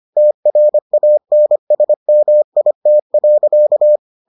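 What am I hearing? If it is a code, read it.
Morse code, heard as a keyed tone, about 25 wpm, 600 Hz. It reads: TRANSMIT.
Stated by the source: Morse edition of the Foundations of Amateur Radio episodes.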